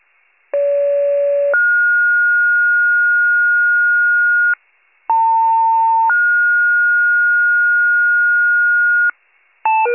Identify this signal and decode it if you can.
Two-tone page, heard as a one-second tone, then a three-second tone, about 570 then 1430 Hz, about 900 then 1430 Hz. It is heard from 0.5 to 9.1 s.